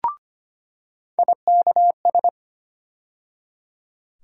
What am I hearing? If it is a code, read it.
Morse code, heard as a keyed tone, about 25 wpm, 700 Hz. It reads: IKS
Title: USA Call Sign Postfixes - 3 Letters 25wpm